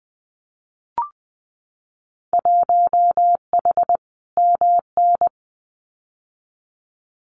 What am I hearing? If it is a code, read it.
Morse code, heard as a keyed tone, about 20 wpm, 700 Hz. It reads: 1H MN
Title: Winter Field Day Exchanges 20wpm